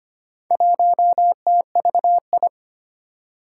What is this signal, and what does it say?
Morse code, heard as a keyed tone, about 25 wpm, 700 Hz. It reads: 1TVI
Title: Four Challenging Letter-Number Combinations 25wpm